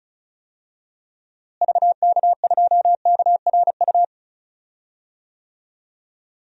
Morse code, heard as a keyed tone, about 36 wpm, 700 Hz. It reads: VK2KRU
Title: DXCC-100 Call Signs - 6 Characters - Repeated 2x - 35wpm